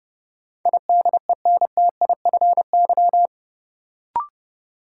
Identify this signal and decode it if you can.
Morse code, heard as a keyed tone, about 30 wpm, 700 Hz. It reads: IDENTIFY
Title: Sentences from Top 900 Words - New Word Drill - 30wpm